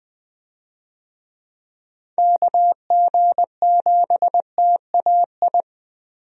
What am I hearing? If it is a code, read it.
Morse code, heard as a keyed tone, about 20 wpm, 700 Hz. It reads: KG7TAI
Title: USA Call Signs - 6 Characters - Repeated 3x - 20wpm